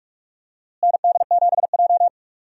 Morse code, heard as a keyed tone, about 46 wpm, 700 Hz. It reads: ND7J
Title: USA Call Signs - 4 Characters - Repeated 3x - 45wpm